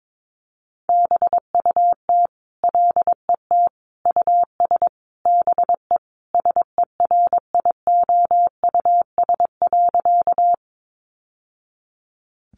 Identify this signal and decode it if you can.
Morse code, heard as a keyed tone, about 22 wpm, 700 Hz. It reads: BUT LET US BE SERIOUS.
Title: Sentences from Top 1100 Words 22wpm